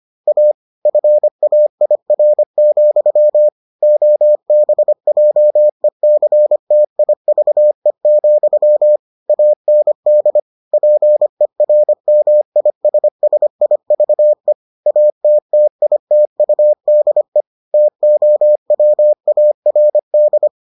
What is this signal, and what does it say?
Morse code, heard as a keyed tone, about 25 wpm, 600 Hz. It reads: A FAIR, OBJECTIVE, AND PERMISSIVE ATTITUDE TOWARD